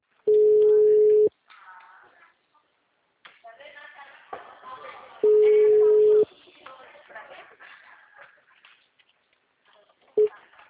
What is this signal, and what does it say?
Ringback tone (Europe, 425 Hz), with speech.